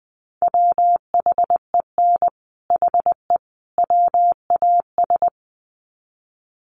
Morse code, heard as a keyed tone, about 20 wpm, 700 Hz. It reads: WHEN HE WAS
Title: Sets of 3 Words 20wpm